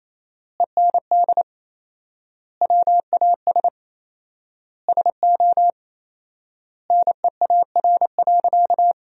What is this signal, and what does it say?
Morse code, heard as a keyed tone, about 27 wpm, 700 Hz. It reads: END WAS SO NEAR.
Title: 4x Word-Spacing - Sentences from Top 300 Words - 28wpm